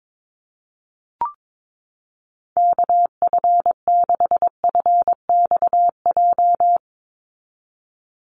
Morse code, heard as a keyed tone, about 22 wpm, 700 Hz. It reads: KF6FXJ
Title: USA Call Signs 22wpm